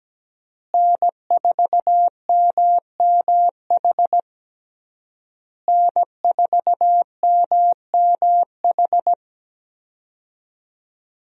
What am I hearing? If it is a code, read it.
Morse code, heard as a keyed tone, about 17 wpm, 700 Hz. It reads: N4MMH N4MMH
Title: USA Call Signs - 5 Characters - Repeated 3x - 17wpm